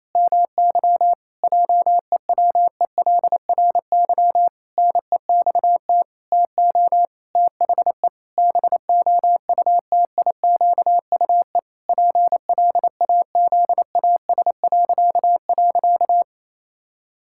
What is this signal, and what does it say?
Morse code, heard as a keyed tone, about 28 wpm, 700 Hz. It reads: MY JEWELRY NEXT TO THE BOUTIQUE PLAZAS..
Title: Pangram Challenge v2 - 28wpm